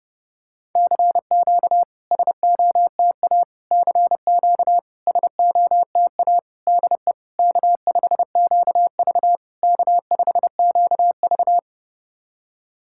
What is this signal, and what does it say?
Morse code, heard as a keyed tone, about 32 wpm, 690 Hz. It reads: CQ SOTA CQ SOTA DE K5QV K5QV
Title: SOTA QSO Practice 30wpm